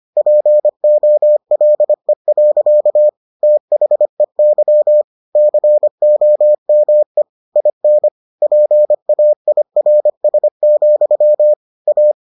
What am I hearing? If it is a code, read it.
Morse code, heard as a keyed tone, about 25 wpm, 600 Hz. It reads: POLE. THEY COME IN PAIRS, A